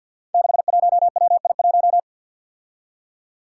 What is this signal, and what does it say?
Morse code, heard as a keyed tone, about 51 wpm, 700 Hz. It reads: 61WI1
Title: Five Challenging Letter-Number Combinations 50wpm